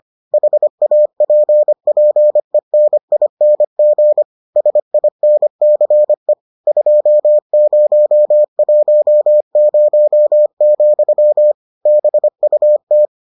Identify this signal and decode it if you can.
Morse code, heard as a keyed tone, about 25 wpm, 600 Hz. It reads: HAPPENING SINCE 2010, BUT